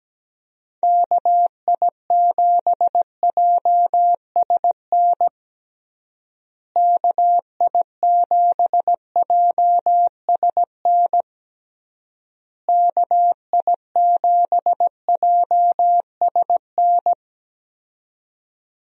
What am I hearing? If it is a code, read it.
Morse code, heard as a keyed tone, about 17 wpm, 700 Hz. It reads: KI7JSN KI7JSN KI7JSN
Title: USA Call Signs - 6 Characters - Repeated 3x - 17wpm